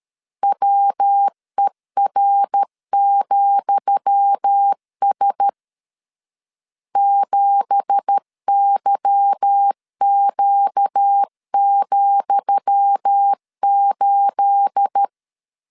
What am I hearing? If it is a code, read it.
Morse code, heard as a keyed tone, about 13 wpm, 800 Hz. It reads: WER,S 7YQ,8
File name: LEZIONI / 10 SIGNS / 10D COMA 12wpm 2544